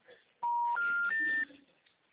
Special information tone.
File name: out- -2006- - - .11937